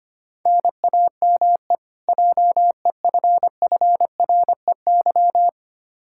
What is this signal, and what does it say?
Morse code, heard as a keyed tone, about 25 wpm, 700 Hz. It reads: NAME JEFFREY
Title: Practice QSO - Exchange 1 - Beginning - 25wpm